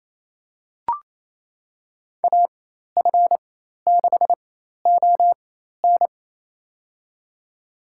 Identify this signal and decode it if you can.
Morse code, heard as a keyed tone, about 28 wpm, spaced as for 13 wpm, 700 Hz. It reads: AF6ON
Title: USA Call Signs - 5 Characters - 4x Character Spacing - 28wpm